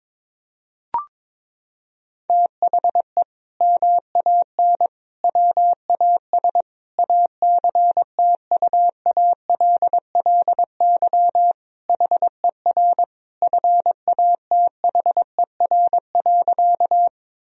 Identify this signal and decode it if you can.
Morse code, heard as a keyed tone, about 21 wpm, 700 Hz. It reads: THE MAN WAS ACTUALLY HER FATHER.
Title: Sentences from Top 1000 Words - Review Part 4 - 22wpm